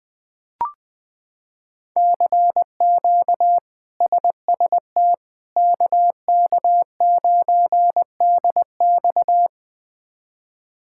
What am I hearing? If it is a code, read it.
Morse code, heard as a keyed tone, about 20 wpm, 700 Hz. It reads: CQ SST KK9DX
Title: K1USN SST QSO Practice - 20wpm